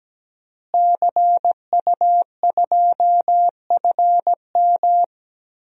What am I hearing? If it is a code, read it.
Morse code, heard as a keyed tone, about 17 wpm, 700 Hz. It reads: CU2FM
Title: DXCC-100 Call Signs - 5 Characters - Repeated 3x - 17wpm